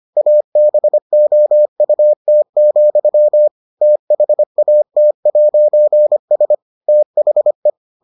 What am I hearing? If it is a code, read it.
Morse code, heard as a keyed tone, about 25 wpm, 600 Hz. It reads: ABOUT, THAT'S THE